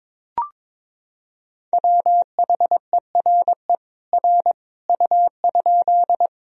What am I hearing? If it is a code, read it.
Morse code, heard as a keyed tone, about 21 wpm, 700 Hz. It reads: WHERE R U?
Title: Sentences from Top 200 Words and Abbreviations 22wpm